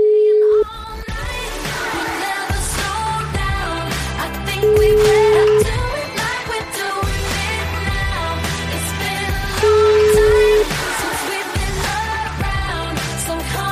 Ringback tone (Europe, 425 Hz), with noise.